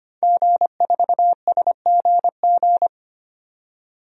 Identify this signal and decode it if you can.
Morse code, heard as a keyed tone, about 25 wpm, 700 Hz. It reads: G4SGG
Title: DXCC-100 Call Signs - 5 Characters - 25wpm